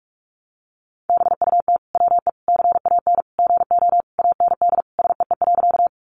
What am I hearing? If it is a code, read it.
Morse code, heard as a keyed tone, about 44 wpm, 700 Hz. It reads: BUT WE CAN GO AND SEE.
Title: Sentences from Top 100 Words 45wpm